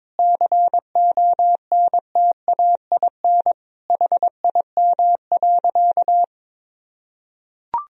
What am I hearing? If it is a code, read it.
Morse code, heard as a keyed tone, about 22 wpm, 700 Hz. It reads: CONTAIN HIM.
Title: Sentences from Top 1000 Words - Review Part 2 - 22wpm